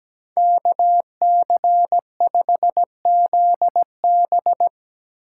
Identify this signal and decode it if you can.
Morse code, heard as a keyed tone, about 17 wpm, 700 Hz. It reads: KC5ZB
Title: USA Call Signs - 5 Characters - 4x Character Spacing - 17wpm